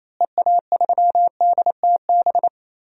Morse code, heard as a keyed tone, about 28 wpm, 700 Hz. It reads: EA3DTB
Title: DXCC-100 Call Signs - 6 Characters - Repeated 3x - 28wpm